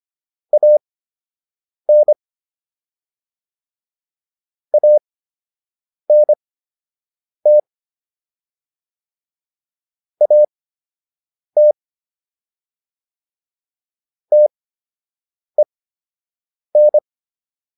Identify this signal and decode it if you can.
Morse code, heard as a keyed tone, about 25 wpm, spaced as for 7 wpm, 600 Hz. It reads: AN ANT AT TEN